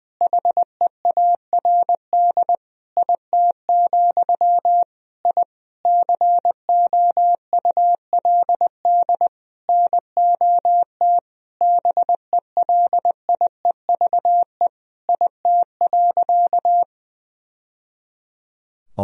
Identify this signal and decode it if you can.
Morse code, heard as a keyed tone, about 20 wpm, 700 Hz. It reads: HEARD IT, I COULD NOT BELIEVE IT.
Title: Sentences from Top 1000 Words - Review Part 3 - 20wpm